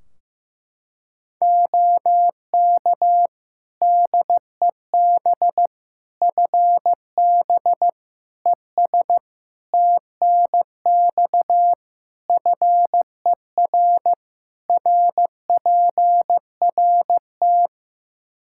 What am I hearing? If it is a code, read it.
Morse code, heard as a keyed tone, about 15 wpm, 700 Hz. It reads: OK DEB FB ES TNX FER RPRT
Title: Practice QSO - Exchange 2 - Info - 15wpm